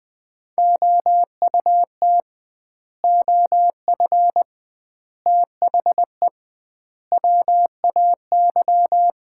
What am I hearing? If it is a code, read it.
Morse code, heard as a keyed tone, about 20 wpm, 700 Hz. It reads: OUT OF THE WAY
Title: 2x Word-Spacing - Sets of 4 Words - 20wpm